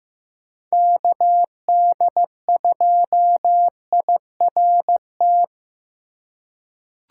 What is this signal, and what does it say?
Morse code, heard as a keyed tone, about 15 wpm, 700 Hz. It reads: KD2IRT
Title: USA Call Signs - 2x Character Spacing - 15wpm